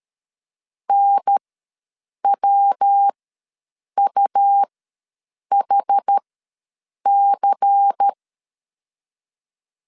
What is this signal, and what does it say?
Morse code, heard as a keyed tone, about 13 wpm, spaced as for 7 wpm, 800 Hz. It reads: NWUHC
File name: LEZIONI / 10 SIGNS / 10D COMA ws 12wpm 3618